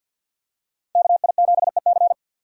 Morse code, heard as a keyed tone, about 49 wpm, 700 Hz. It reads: KI7EC
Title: USA Call Signs - 5 Characters - 4x Character Spacing - 50wpm